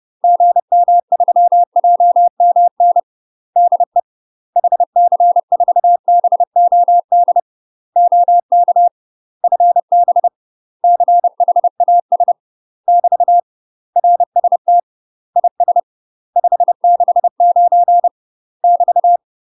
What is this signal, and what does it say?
Morse code, heard as a keyed tone, about 30 wpm, 700 Hz. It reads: GM3JMN DE HC4BOD OK FB CHAS <BT> RST IS 569 <BT>
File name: UK MASTER QSOs TRAINING / 30WPM / UK MASTER QSOs 0008